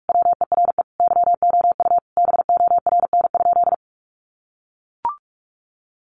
Morse code, heard as a keyed tone, about 44 wpm, 700 Hz. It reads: WERE YOU BORN?
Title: Sentences from Top 1000 Words 45wpm